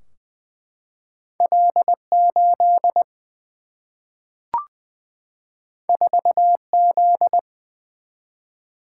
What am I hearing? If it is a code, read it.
Morse code, heard as a keyed tone, about 20 wpm, 700 Hz. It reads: L 8 4 Z
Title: Single-Letter Single-Number 20wpm